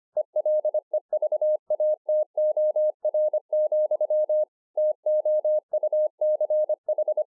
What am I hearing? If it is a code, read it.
Morse code, heard as a keyed tone, about 25 wpm, 610 Hz. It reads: ELEVATOR, TOUCH